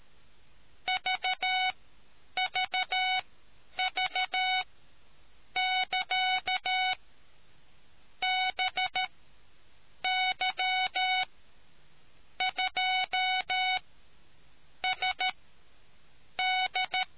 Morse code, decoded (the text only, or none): VVV<KA>BY2SD